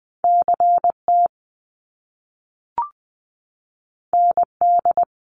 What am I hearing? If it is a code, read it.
Morse code, heard as a keyed tone, about 20 wpm, 700 Hz. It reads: CT ND
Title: US States 20wpm